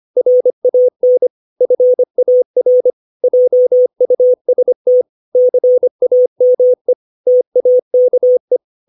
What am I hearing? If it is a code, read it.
Morse code, heard as a keyed tone, about 25 wpm, 500 Hz. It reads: RAN FAR JUST CAME TAKE
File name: 632 words / 632 Words 25WPM